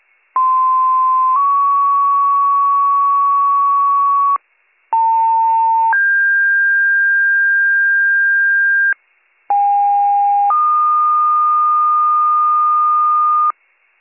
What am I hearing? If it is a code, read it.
Two-tone page, heard as a one-second tone, then a three-second tone, about 1040 then 1120 Hz, about 900 then 1600 Hz, about 810 then 1220 Hz. It is heard from 0.4 to 13.5 s.